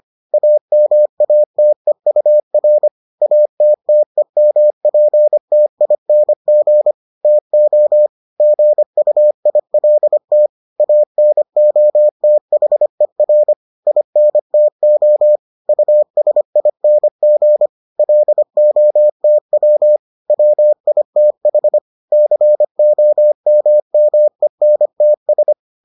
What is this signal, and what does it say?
Morse code, heard as a keyed tone, about 25 wpm, 600 Hz. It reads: AMATEUR ATTEMPTING TO GUILT ANOTHER INTO USING LOTW WITH COMMENTS